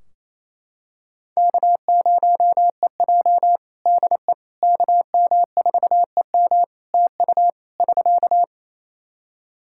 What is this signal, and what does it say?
Morse code, heard as a keyed tone, about 27 wpm, 700 Hz. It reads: K0EJ DE KM4EM TU <SK>
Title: Practice QSO - Exchange 4 - Ending - 28wpm